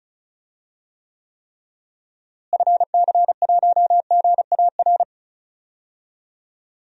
Morse code, heard as a keyed tone, about 34 wpm, 700 Hz. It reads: FC1GAR
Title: DXCC-100 Call Signs - 6 Characters - Repeated 3x - 35wpm